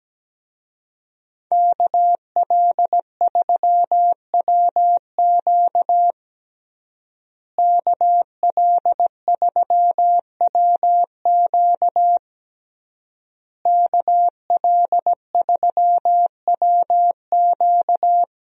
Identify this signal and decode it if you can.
Morse code, heard as a keyed tone, about 17 wpm, 700 Hz. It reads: KL3WQ KL3WQ KL3WQ
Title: USA Call Signs - 5 Characters - Repeated 3x - 17wpm